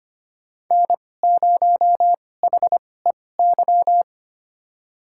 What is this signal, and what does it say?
Morse code, heard as a keyed tone, about 25 wpm, spaced as for 18 wpm, 700 Hz. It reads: N0HEY
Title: USA Call Signs - 5 Characters - 2x Character Spacing - 25wpm